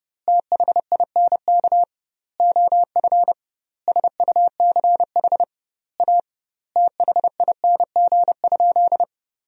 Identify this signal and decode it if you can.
Morse code, heard as a keyed tone, about 30 wpm, 700 Hz. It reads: THINK OF SUCH A THING?
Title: 2x Word-Spacing - Sentences from Top 200 Words - 30wpm